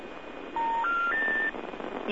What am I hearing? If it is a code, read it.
Special information tone, with noise.